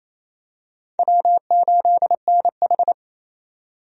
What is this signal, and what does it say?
Morse code, heard as a keyed tone, about 28 wpm, 700 Hz. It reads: W8NH
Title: USA Call Signs - 4 Characters - 28wpm